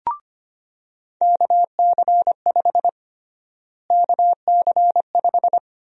Morse code, heard as a keyed tone, about 25 wpm, 700 Hz. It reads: KC5 KC5
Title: USA Call Sign Prefixes - Repeated 2x - 25wpm